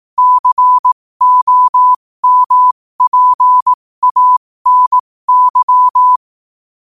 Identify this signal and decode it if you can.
Morse code, heard as a keyed tone, about 18 wpm, spaced as for 15 wpm, 1000 Hz. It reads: COMPANY
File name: 15wpm